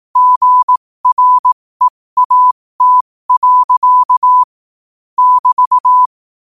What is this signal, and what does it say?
Morse code, heard as a keyed tone, about 18 wpm, spaced as for 15 wpm, 1000 Hz. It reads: GREAT. <BT>
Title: 15 WPM morse code quotes for Fri, 01 Mar 2024